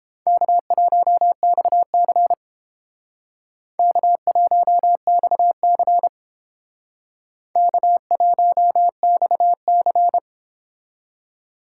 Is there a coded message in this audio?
Yes: Morse code from 0.3 to 10.2 s, heard as a keyed tone, about 30 wpm, 700 Hz. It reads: K1XC K1XC K1XC